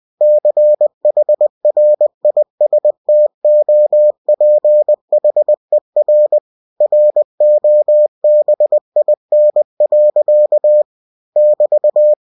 Morse code, heard as a keyed tone, about 20 wpm, 600 Hz. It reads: CHRISTOPHER ROBIN. <BT>
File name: cw-books / winnie-the-pooh / winnie-20wpm-page-0084